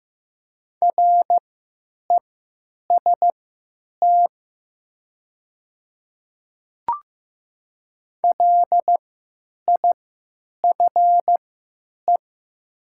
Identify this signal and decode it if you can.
Morse code, heard as a keyed tone, about 15 wpm, spaced as for 9 wpm, 700 Hz. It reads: REST LIFE
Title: 3x Character Spacing - Top 500 Words - 15wpm